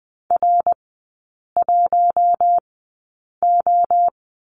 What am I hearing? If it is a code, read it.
Morse code, heard as a keyed tone, about 20 wpm, spaced as for 8 wpm, 690 Hz. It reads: R1O
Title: 093 Lesson 16e - Letter c - Three Characters 20wpm